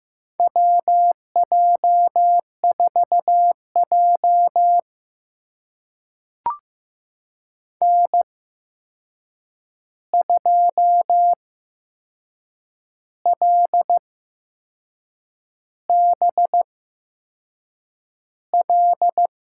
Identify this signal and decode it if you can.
Morse code, heard as a keyed tone, about 15 wpm, spaced as for 6 wpm, 700 Hz. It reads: WJ4J N2LBL
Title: USA Call Signs - 8x Character Spacing - 15wpm